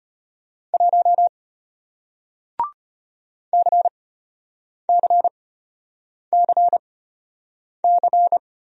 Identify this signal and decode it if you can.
Morse code, heard as a keyed tone, about 33 wpm, spaced as for 8 wpm, 700 Hz. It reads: 1 CCCC